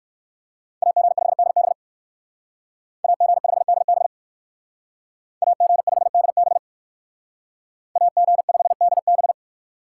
Morse code, heard as a keyed tone, about 58 wpm, 700 Hz. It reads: AG5DB AG5DB AG5DB AG5DB